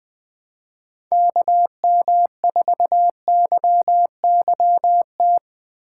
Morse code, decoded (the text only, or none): KM4YYT